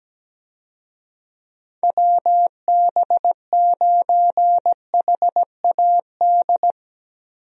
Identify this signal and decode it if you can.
Morse code, heard as a keyed tone, about 17 wpm, 700 Hz. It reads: WB9HAD
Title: USA Call Signs - 6 Characters - Repeated 3x - 17wpm